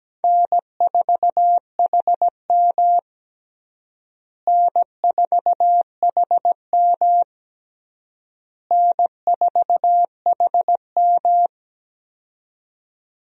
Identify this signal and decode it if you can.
Morse code, heard as a keyed tone, about 17 wpm, 700 Hz. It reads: N4HM N4HM N4HM
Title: USA Call Signs - 4 Characters - Repeated 3x - 17wpm